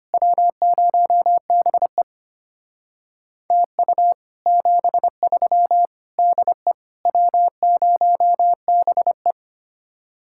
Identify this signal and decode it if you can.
Morse code, heard as a keyed tone, about 32 wpm, 700 Hz. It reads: W0BE TU 73 DE W0BE